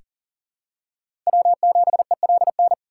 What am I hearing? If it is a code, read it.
Morse code, heard as a keyed tone, about 40 wpm, 700 Hz. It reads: W7ELN
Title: USA Call Signs - 4x Character Spacing - 40wpm